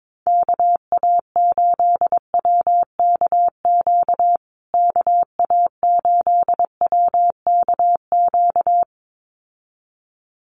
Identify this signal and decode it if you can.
Morse code, heard as a keyed tone, about 22 wpm, 700 Hz. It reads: KA8WKQ KA8WKQ
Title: USA Call Signs Repeated 2x 22wpm